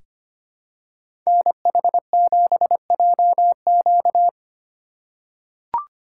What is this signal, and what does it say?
Morse code, heard as a keyed tone, about 25 wpm, 700 Hz. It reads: NH7JQ
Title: USA Call Signs - 5 Characters - 3x Character Spacing - 25wpm